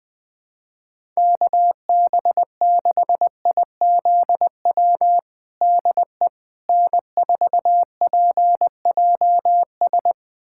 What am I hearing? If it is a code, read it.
Morse code, heard as a keyed tone, about 21 wpm, 700 Hz. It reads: KB6IZW DE N4PJS
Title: USA Callsign DE Callsign 20wpm